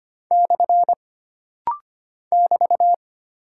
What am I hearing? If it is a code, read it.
Morse code, heard as a keyed tone, about 25 wpm, 700 Hz. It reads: / <BT>
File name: Single Punctuation-Prosign - Rapid-Fire - 25wpm